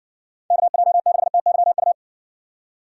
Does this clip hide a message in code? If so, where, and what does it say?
Yes: Morse code from 0.5 to 1.9 s, heard as a keyed tone, about 60 wpm, 700 Hz. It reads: KJ6TYU